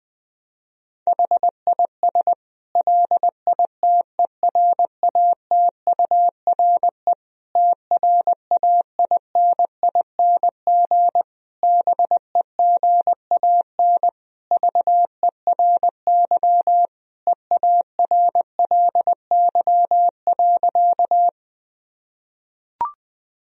Morse code, heard as a keyed tone, about 21 wpm, 700 Hz. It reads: HIS LITERATURE TRAINING BEGAN VERY EARLY.
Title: Sentences from Top 1000 Words - Review Part 4 - 20wpm